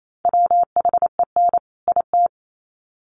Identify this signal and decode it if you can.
Morse code, heard as a keyed tone, about 28 wpm, 700 Hz. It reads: WHEN IT
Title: Sets of 2 Words 28wpm